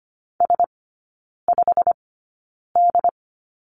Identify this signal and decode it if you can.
Morse code, heard as a keyed tone, about 25 wpm, spaced as for 9 wpm, 700 Hz. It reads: S5D
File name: Lesson 15e - Number 5 - Three Characters 25wpm